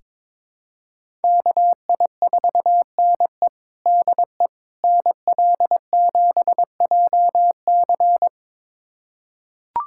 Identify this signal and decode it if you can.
Morse code, heard as a keyed tone, about 22 wpm, 700 Hz. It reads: KI4NE DE NL7JC
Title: USA Callsign DE Callsign 22wpm